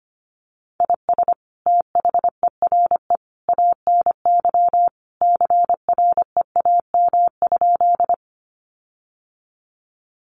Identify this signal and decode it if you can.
Morse code, heard as a keyed tone, about 25 wpm, 700 Hz. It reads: IS THERE ANY CREAM?
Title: Sentences from Top 1500 Words 25wpm